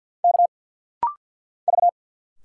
Morse code, heard as a keyed tone, about 50 wpm, 690 Hz. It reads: K V
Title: Single Letters - Rapid-Fire 50wpm